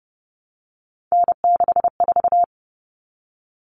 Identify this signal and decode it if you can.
Morse code, heard as a keyed tone, about 28 wpm, 700 Hz. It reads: N64
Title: Common Cut Numbers - 3-Digit 30wpm